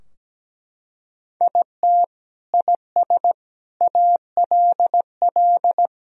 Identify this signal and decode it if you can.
Morse code, heard as a keyed tone, about 17 wpm, 700 Hz. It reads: IT IS ALL
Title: Sentences from Top 1000 Words - Review Part 4 - 17wpm